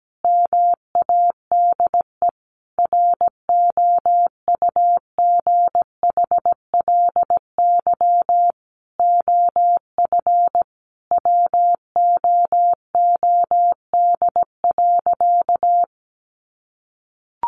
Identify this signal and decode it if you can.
Morse code, heard as a keyed tone, about 17 wpm, 700 Hz. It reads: MADE ROUGHLY OF WOOD.
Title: Sentences from Top 1500 Words 17wpm